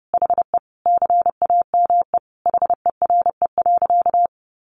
Morse code, heard as a keyed tone, about 30 wpm, 700 Hz. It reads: HE CAME HERE.
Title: Sentences from Top 500 Words 30wpm